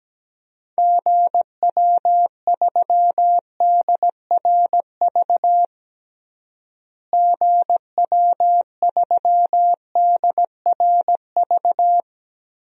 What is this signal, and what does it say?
Morse code, heard as a keyed tone, about 17 wpm, 700 Hz. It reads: GW3DRV GW3DRV
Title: DXCC-100 Call Signs - 6 Characters - Repeated 3x - 17wpm